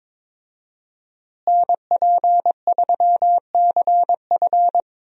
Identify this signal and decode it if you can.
Morse code, heard as a keyed tone, about 22 wpm, 700 Hz. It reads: NP3CF
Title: USA Call Signs - 5 Characters - Repeated 2x - 22wpm